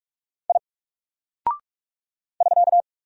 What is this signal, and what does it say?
Morse code, heard as a keyed tone, about 44 wpm, 700 Hz. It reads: I <SK>
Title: Single Character (Letter:Number:Punctuation:Prosign) 45wpm